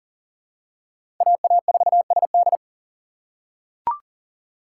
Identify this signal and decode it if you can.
Morse code, heard as a keyed tone, about 40 wpm, 700 Hz. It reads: AA4SD
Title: USA Call Signs - 5 Characters - 2x Character Spacing - 40wpm